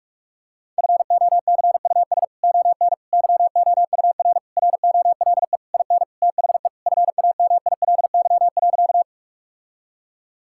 Morse code, heard as a keyed tone, about 41 wpm, 700 Hz. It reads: FOCUS ON YOUR ROLE IN THE FAMILY.
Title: Sentences from Top 1000 Words - Review Part 6 - 45wpm